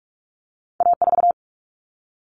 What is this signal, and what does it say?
Morse code, heard as a keyed tone, about 46 wpm, 700 Hz. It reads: A4
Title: All Cut Numbers - 2-Digit 45wpm